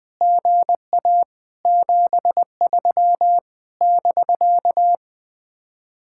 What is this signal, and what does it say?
Morse code, heard as a keyed tone, about 20 wpm, 700 Hz. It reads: GA 73 <BK>
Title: Speed-Racing Common QSO Phrases - 17wpm